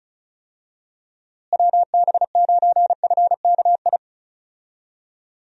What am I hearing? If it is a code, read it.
Morse code, heard as a keyed tone, about 36 wpm, 700 Hz. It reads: WB9FKI
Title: USA Call Signs - 6 Characters - Repeated 3x - 35wpm